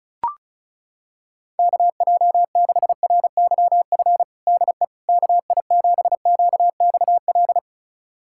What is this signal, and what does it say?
Morse code, heard as a keyed tone, about 36 wpm, 700 Hz. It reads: KJ6RYF DE KI7QXL
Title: USA Callsign DE Callsign - 6 Characters - 35wpm